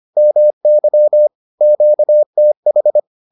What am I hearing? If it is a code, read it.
Morse code, heard as a keyed tone, about 25 wpm, 600 Hz. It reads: MY QTH